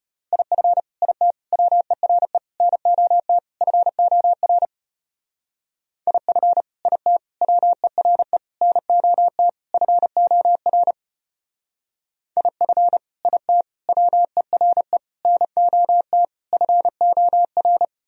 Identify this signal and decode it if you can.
Morse code, heard as a keyed tone, about 29 wpm, 700 Hz. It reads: IF IT WERE NOT FOR IF IT WERE NOT FOR IF IT WERE NOT FOR
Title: Speed-Racing - Sets of 5 Words 25wpm